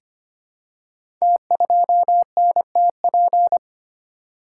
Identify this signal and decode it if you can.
Morse code, heard as a keyed tone, about 25 wpm, 700 Hz. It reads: T2NTP
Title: Code Groups - 5 Letter:Numbers - 25wpm